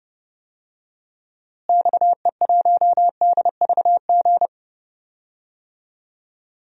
Morse code, heard as a keyed tone, about 30 wpm, 700 Hz. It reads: XE1DVG
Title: DXCC-100 Call Signs - 6 Characters - Repeated 3x - 30wpm